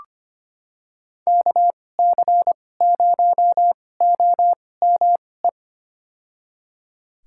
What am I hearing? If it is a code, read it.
Morse code, heard as a keyed tone, about 25 wpm, spaced as for 18 wpm, 700 Hz. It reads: KC0OME
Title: USA Call Signs - 6 Characters - 2x Character Spacing - 25wpm